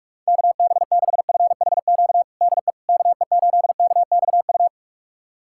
Morse code, heard as a keyed tone, about 46 wpm, 700 Hz. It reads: KB6FHQ DE KE8KXU
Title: USA Callsign DE Callsign 45wpm